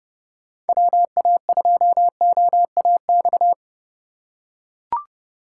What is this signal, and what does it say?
Morse code, heard as a keyed tone, about 32 wpm, 700 Hz. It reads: WA2OAX
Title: USA Call Signs - 6 Characters - 3x Character Spacing - 30wpm